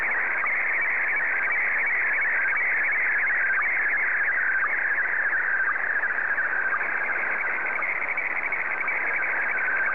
SSTV (slow-scan television), Scottie DX.